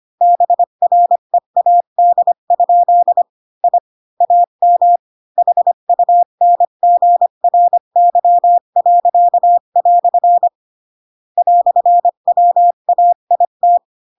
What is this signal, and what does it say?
Morse code, heard as a keyed tone, about 25 wpm, 700 Hz. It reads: BREAD? I AM HUNGRY." "WAIT